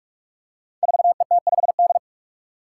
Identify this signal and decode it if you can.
Morse code, heard as a keyed tone, about 44 wpm, 700 Hz. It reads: 4ET5D
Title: Five Challenging Letter-Number Combinations 45wpm